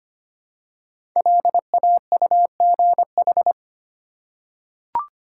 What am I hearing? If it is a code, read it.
Morse code, heard as a keyed tone, about 25 wpm, 700 Hz. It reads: LAUGH
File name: 5 Letter Words - First 3 Letters Given - 25wpm